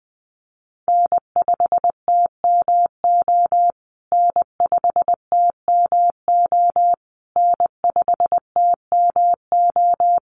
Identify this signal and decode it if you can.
Morse code, heard as a keyed tone, about 20 wpm, 700 Hz. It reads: N5TMO N5TMO N5TMO
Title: USA Call Signs Repeated 3x 20wpm